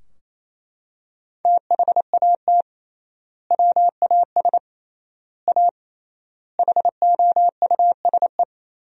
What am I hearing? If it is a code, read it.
Morse code, heard as a keyed tone, about 27 wpm, 700 Hz. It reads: THAT WAS A HOUSE
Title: 3x Word-Spacing - Sentences from Top 200 Words - 28wpm